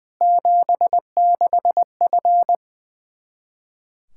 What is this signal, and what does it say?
Morse code, heard as a keyed tone, about 21 wpm, 700 Hz. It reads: 76F
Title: Practice QSO - Exchange 2 - Info - 20wpm